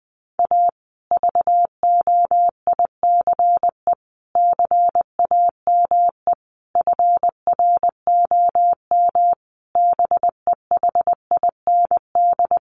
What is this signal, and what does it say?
Morse code, heard as a keyed tone, about 20 wpm, 700 Hz. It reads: A VOICE CAME FROM BEHIND
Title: Sentences from Top 1400 Words 20wpm